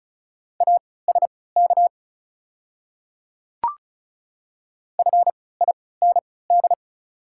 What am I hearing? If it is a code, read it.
Morse code, heard as a keyed tone, about 36 wpm, spaced as for 20 wpm, 700 Hz. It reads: ASK FIND
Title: 3x Character Spacing - Top 200 Words - 35wpm